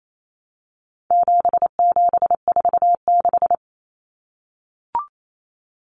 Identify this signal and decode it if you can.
Morse code, heard as a keyed tone, about 28 wpm, 700 Hz. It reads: 7746
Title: Numbers - 4-Digit 28wpm